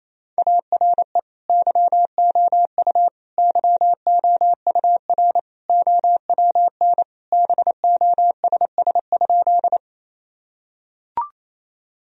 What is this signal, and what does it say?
Morse code, heard as a keyed tone, about 28 wpm, 700 Hz. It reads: ARE YOU YOUR OWN BOSS?
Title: Sentences from Top 1000 Words - Review Part 4 - 28wpm